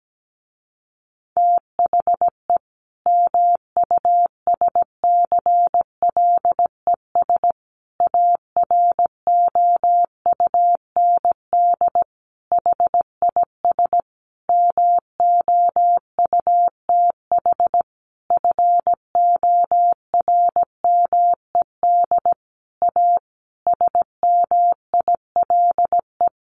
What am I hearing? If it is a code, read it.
Morse code, heard as a keyed tone, about 17 wpm, 700 Hz. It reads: THE MUSCLES AROUND HIS MOUTH FORMED A SMILE
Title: A Sound Of Thunder 17wpm